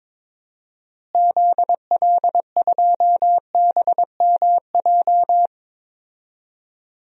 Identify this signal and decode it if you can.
Morse code, heard as a keyed tone, about 22 wpm, 700 Hz. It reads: ZL2BMJ
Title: DXCC-100 Call Signs - 6 Characters - Repeated 3x - 22wpm